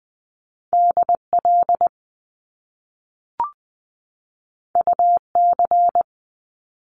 Morse code, heard as a keyed tone, about 20 wpm, 700 Hz. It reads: DL UC
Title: Top Two Letter Combinations in Words 20wpm